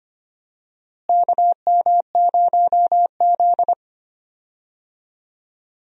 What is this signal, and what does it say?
Morse code, heard as a keyed tone, about 25 wpm, 700 Hz. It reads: KM0Z
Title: USA Call Signs - 4 Characters - Repeated 3x - 25wpm